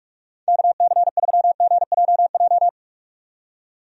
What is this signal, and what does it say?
Morse code, heard as a keyed tone, about 46 wpm, 700 Hz. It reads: KC3GJJ
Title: USA Call Signs - 6 Characters - 45wpm